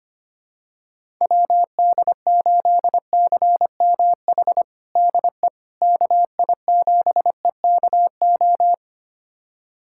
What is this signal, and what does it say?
Morse code, heard as a keyed tone, about 25 wpm, 700 Hz. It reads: WD8CMH DE KI7EKO
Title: USA Callsign DE Callsign 25wpm